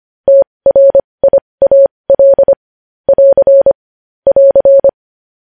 Morse code, heard as a keyed tone, about 25 wpm, spaced as for 20 wpm, 550 Hz. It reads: TRIAL <AR> <AR>
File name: podcast / News headlines-1x-25x20